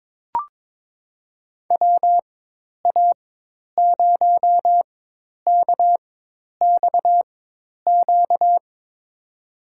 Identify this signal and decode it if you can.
Morse code, heard as a keyed tone, about 21 wpm, spaced as for 10 wpm, 700 Hz. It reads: WA0KXQ
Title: USA Call Signs - 4x Character Spacing - 22wpm